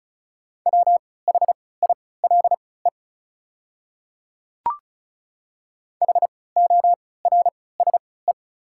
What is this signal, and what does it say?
Morse code, heard as a keyed tone, about 34 wpm, spaced as for 20 wpm, 690 Hz. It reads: WHILE HORSE
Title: 3x Character Spacing - Top 300 Words - 35wpm